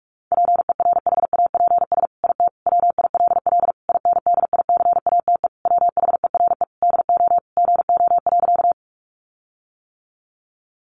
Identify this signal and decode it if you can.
Morse code, heard as a keyed tone, about 44 wpm, 700 Hz. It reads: PERHAPS IT WILL INDICATE WHERE DO GO.
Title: Sentences from Top 1300 Words 45wpm